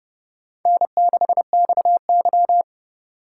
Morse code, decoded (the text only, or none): N6XY